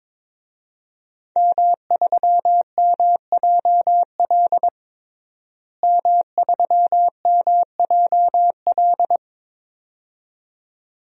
Morse code, heard as a keyed tone, about 22 wpm, 700 Hz. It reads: M3MJL M3MJL
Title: DXCC-100 Call Signs - 5 Characters - Repeated 2x - 22wpm